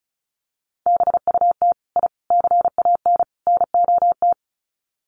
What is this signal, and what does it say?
Morse code, heard as a keyed tone, about 34 wpm, 700 Hz. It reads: BUT I CAN NOT
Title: Sets of 4 Words 35wpm